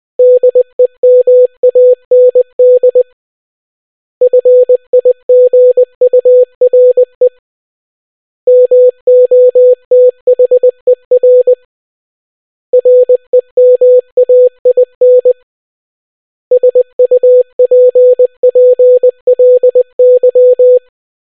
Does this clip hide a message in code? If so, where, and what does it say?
Yes: Morse code from 0.2 to 20.8 s, heard as a keyed tone, about 20 wpm, 500 Hz. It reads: DEMAND FIGURE MOTHER REMAIN SUPPLY